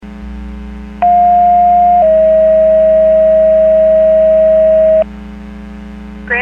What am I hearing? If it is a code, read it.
Two-tone page, heard as a one-second tone, then a three-second tone, about 710 then 640 Hz, with speech. It is heard from 1.0 to 5.0 s.